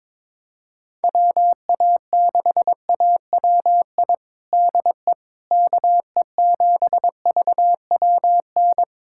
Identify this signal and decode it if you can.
Morse code, heard as a keyed tone, about 22 wpm, 700 Hz. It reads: WA6AWI DE KE7VWN
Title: USA Callsign DE Callsign - 6 Characters - 22wpm